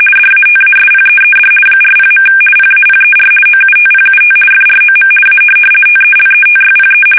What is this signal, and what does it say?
RTTY, about 50 baud. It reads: DATASPEED 40. PAPER OUTPUT FROM A TELETYPE
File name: fsk850-50B / teletype-corp